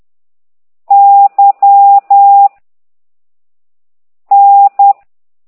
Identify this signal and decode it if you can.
Morse code, heard as a keyed tone, about 10 wpm, 800 Hz. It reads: Y N
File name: ct2idw / exerc4